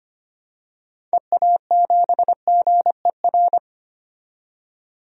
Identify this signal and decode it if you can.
Morse code, heard as a keyed tone, about 25 wpm, 700 Hz. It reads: EA7GER
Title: DXCC-100 Call Signs - 6 Characters - 25wpm